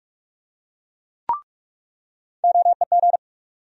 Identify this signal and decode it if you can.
Morse code, heard as a keyed tone, about 44 wpm, 700 Hz. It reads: OEG
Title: USA Call Sign Postfixes - 3 Letters 45wpm